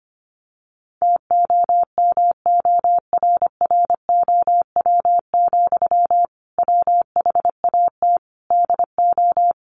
Morse code, heard as a keyed tone, about 25 wpm, 700 Hz. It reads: TOMORROW, WHAT DO
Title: Sentences from Top 1500 Words 25wpm